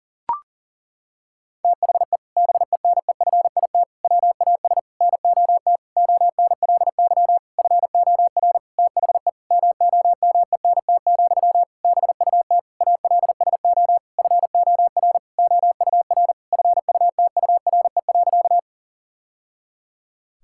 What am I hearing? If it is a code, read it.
Morse code, heard as a keyed tone, about 40 wpm, 700 Hz. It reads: THE BENEFIT WAS NOT ONLY FOR THE MOMENT, BUT ALSO FOR OUR FUTURE.
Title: Sentences from Top 1000 Words - Review Part 6 - 40wpm